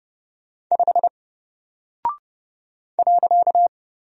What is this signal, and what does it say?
Morse code, heard as a keyed tone, about 30 wpm, 700 Hz. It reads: 5 .
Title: Single Character (Letter:Number:Punctuation:Prosign) Alt 30wpm